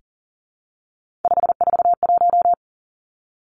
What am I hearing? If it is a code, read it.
Morse code, heard as a keyed tone, about 41 wpm, 700 Hz. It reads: 541
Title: Three Challenging Letter-Number Combinations 40wpm